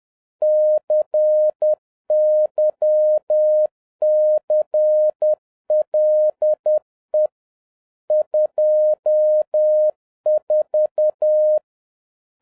Morse code, heard as a keyed tone, about 10 wpm, 610 Hz. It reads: CYCLE 24